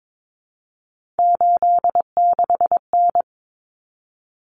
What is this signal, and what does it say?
Morse code, heard as a keyed tone, about 22 wpm, 700 Hz. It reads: 86N
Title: Common Cut Numbers - 3-Digit 22wpm